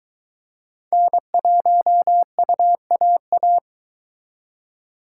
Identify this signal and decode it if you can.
Morse code, heard as a keyed tone, about 22 wpm, 700 Hz. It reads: N1UAA